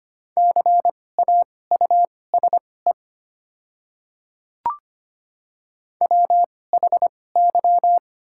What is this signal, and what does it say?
Morse code, heard as a keyed tone, about 25 wpm, spaced as for 18 wpm, 700 Hz. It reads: CAUSE WHY
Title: 2x Character Spacing - Top 200 Words - 25wpm